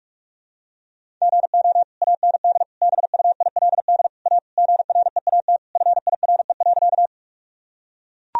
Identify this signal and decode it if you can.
Morse code, heard as a keyed tone, about 42 wpm, 700 Hz. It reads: GO AND BUILD A GREAT FIRE.
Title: Sentences from Top 1000 Words - Review Part 1 - 45wpm